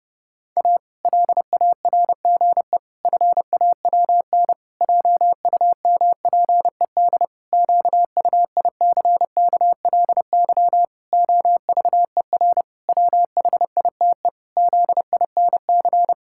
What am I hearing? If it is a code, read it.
Morse code, heard as a keyed tone, about 30 wpm, 700 Hz. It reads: A LARGE FAWN JUMPED QUICKLY OVER WHITE ZINC
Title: Pangram Challenge v2 - 30wpm